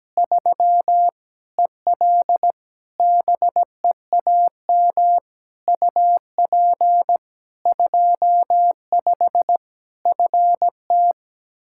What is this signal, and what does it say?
Morse code, heard as a keyed tone, about 17 wpm, 700 Hz. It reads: 3 EL BEAM UP 25 FT
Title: Practice QSO - Exchange 2 - Info - 17wpm